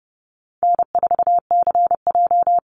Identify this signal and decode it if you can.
Morse code, heard as a keyed tone, about 30 wpm, 700 Hz. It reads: N4CJ
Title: Easy Call Signs Repeated 2x - 30wpm